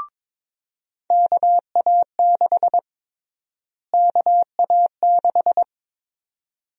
Morse code, heard as a keyed tone, about 21 wpm, 700 Hz. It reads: KA6 KA6
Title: USA Call Sign Prefixes - Repeated 2x - 22wpm